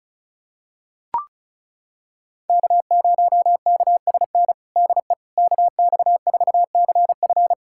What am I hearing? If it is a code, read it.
Morse code, heard as a keyed tone, about 36 wpm, 700 Hz. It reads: K0KSN DE KX4CF
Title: USA Callsign DE Callsign 35wpm